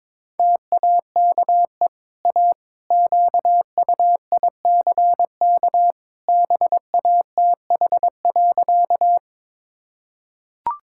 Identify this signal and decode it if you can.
Morse code, heard as a keyed tone, about 22 wpm, 700 Hz. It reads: TAKE A QUICK BATH.
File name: practice / 22wpm / Easy Sentences with Hints - 22wpm